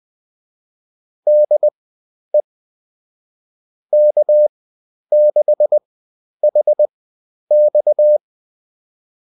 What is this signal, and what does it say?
Morse code, heard as a keyed tone, about 20 wpm, spaced as for 10 wpm, 600 Hz. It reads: DE K6HX